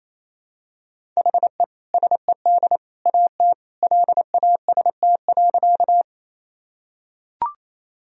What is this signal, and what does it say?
Morse code, heard as a keyed tone, about 28 wpm, 700 Hz. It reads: HE SED AT LAST.
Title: Sentences from Top 300 Words and Abbreviations 28wpm